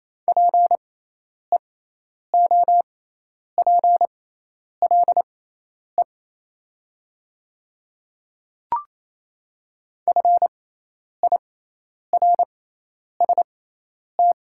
Morse code, heard as a keyed tone, about 28 wpm, spaced as for 10 wpm, 700 Hz. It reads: PEOPLE FIRST
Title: 6x Character Spacing - Top 200 Words - 28wpm